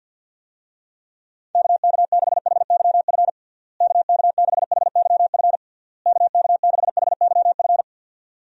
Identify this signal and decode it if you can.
Morse code, heard as a keyed tone, about 53 wpm, 700 Hz. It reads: KK6HYF KK6HYF KK6HYF
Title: USA Call Signs - 6 Characters - Repeated 3x - 50wpm